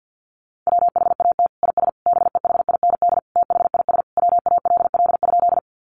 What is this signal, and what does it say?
Morse code, heard as a keyed tone, about 50 wpm, 700 Hz. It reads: WHAT IS BEHIND THIS WALL?
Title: Sentences from Top 1100 Words 50wpm